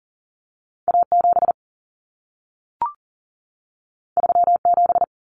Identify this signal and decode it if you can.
Morse code, heard as a keyed tone, about 40 wpm, 700 Hz. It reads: A 7 3 7
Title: Common Cut Numbers - 2-Digit 40wpm